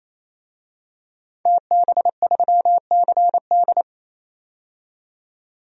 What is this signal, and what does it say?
Morse code, heard as a keyed tone, about 28 wpm, 700 Hz. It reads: TB3CD
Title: DXCC-100 Call Signs - 5 Characters - 28wpm